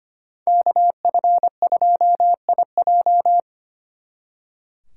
Morse code, heard as a keyed tone, about 31 wpm, 700 Hz. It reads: KF2IJ